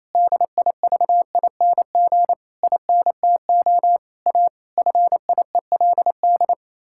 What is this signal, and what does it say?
Morse code, heard as a keyed tone, about 27 wpm, 700 Hz. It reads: DIVING INTO A FIELD
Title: Pangram Challenge v2 - 28wpm